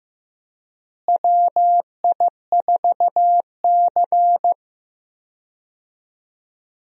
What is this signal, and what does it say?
Morse code, heard as a keyed tone, about 15 wpm, 700 Hz. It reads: WI4C
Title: USA Call Signs - 4 Characters - Repeated 3x - 15wpm